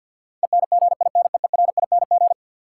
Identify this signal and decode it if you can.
Morse code, heard as a keyed tone, about 53 wpm, 700 Hz. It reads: ENGINEERING
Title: Sentences from Top 1500 Words - New Word Drill - 50wpm